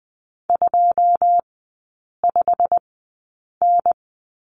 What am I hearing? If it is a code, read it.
Morse code, heard as a keyed tone, about 20 wpm, spaced as for 8 wpm, 700 Hz. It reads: 25N
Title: 087 Lesson 15e - Number 5 - Three Characters 20wpm